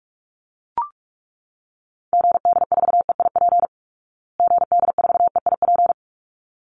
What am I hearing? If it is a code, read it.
Morse code, heard as a keyed tone, about 44 wpm, 700 Hz. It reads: GD4EIP GD4EIP
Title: Call Signs Repeated 3x - 45wpm